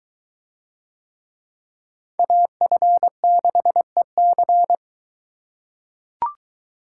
Morse code, heard as a keyed tone, about 21 wpm, 700 Hz. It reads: AF6EC